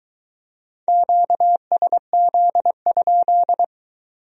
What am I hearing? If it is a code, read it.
Morse code, heard as a keyed tone, about 22 wpm, 700 Hz. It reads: QSZ?